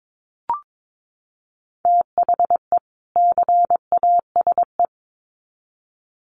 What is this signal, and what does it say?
Morse code, heard as a keyed tone, about 22 wpm, 700 Hz. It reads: THE CASE
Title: Sets of 2 Words 22wpm